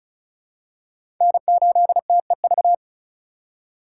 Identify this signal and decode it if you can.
Morse code, heard as a keyed tone, about 36 wpm, 700 Hz. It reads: N8TEV
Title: USA Call Signs - 5 Characters - 4x Character Spacing - 35wpm